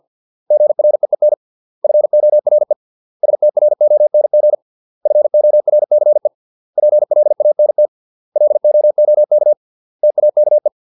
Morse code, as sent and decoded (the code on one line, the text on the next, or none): --. .-. . -. / ..-. --- .-. / ... - .-. --- -. --. / ..-. --- .-. -.-. / .--. .-.. .- -. - / .-.. --- --- -.- / - .- -.- .
GREEN FORE STRONG FORCE PLANT LOOK TAKE